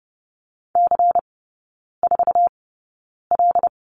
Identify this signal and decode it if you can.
Morse code, heard as a keyed tone, about 30 wpm, spaced as for 9 wpm, 690 Hz. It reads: C4L
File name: Lesson 16e - Letter c - Three Characters 30wpm